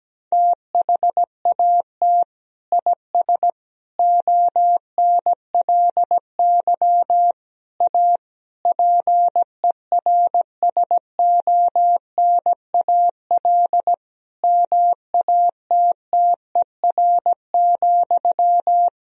Morse code, heard as a keyed tone, about 17 wpm, 700 Hz. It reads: THAT IS ONLY A PERSONAL MATTER,
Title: Sentences from Top 1000 Words - Review Part 4 - 17wpm